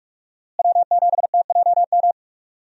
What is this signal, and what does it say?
Morse code, heard as a keyed tone, about 46 wpm, 700 Hz. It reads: W7TJM
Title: USA Call Signs - 5 Characters - Repeated 3x - 45wpm